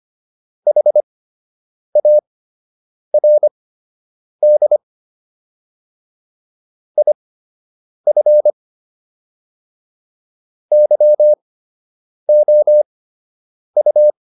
Morse code, heard as a keyed tone, about 25 wpm, spaced as for 8 wpm, 600 Hz. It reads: HARD IF YOU